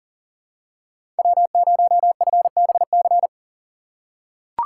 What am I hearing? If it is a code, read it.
Morse code, heard as a keyed tone, about 40 wpm, 700 Hz. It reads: W0FBC
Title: USA Call Signs - 5 Characters - 6x Character Spacing - 40wpm